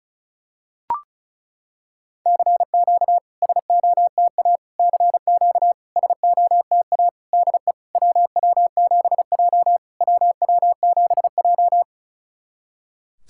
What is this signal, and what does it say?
Morse code, heard as a keyed tone, about 36 wpm, 700 Hz. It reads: CQ SOTA CQ SOTA DE WW7J WW7J
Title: SOTA QSO Practice 35wpm